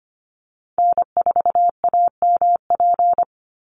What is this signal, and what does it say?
Morse code, heard as a keyed tone, about 25 wpm, 700 Hz. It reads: N4AMP
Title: USA Call Signs 25wpm